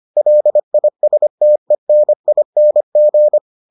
Morse code, heard as a keyed tone, about 25 wpm, 600 Hz. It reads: LISTENING